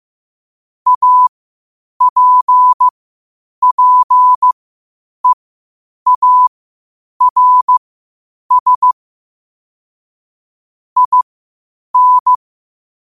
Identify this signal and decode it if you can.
Morse code, heard as a keyed tone, about 15 wpm, spaced as for 8 wpm, 1000 Hz. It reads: APPEARS IN
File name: qotdfiles / 8wpm